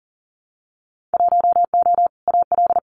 Morse code, heard as a keyed tone, about 39 wpm, 700 Hz. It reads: 1O AL
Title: Winter Field Day Exchanges 40wpm